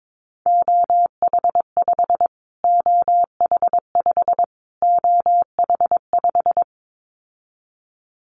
Morse code, heard as a keyed tone, about 22 wpm, 700 Hz. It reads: OH5 OH5 OH5
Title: Callsign Prefixes for Top 25 DXCC - Repeated 3x - 22wpm